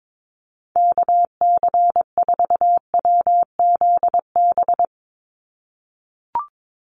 Morse code, heard as a keyed tone, about 22 wpm, 700 Hz. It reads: KC4WZB
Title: NA SOTA Activator Callsigns 22wpm